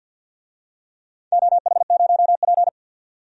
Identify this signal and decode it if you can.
Morse code, heard as a keyed tone, about 47 wpm, 700 Hz. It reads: OH0P
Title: DXCC-100 Call Signs - 4 Characters - 50wpm